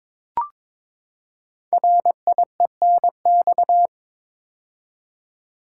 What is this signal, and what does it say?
Morse code, heard as a keyed tone, about 22 wpm, 700 Hz. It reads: RIENX
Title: Code Groups - 5 Letters - 22wpm